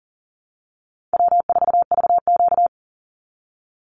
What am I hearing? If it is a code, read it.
Morse code, heard as a keyed tone, about 40 wpm, 700 Hz. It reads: W4VQ
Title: Easy Call Signs Repeated 2x - 40wpm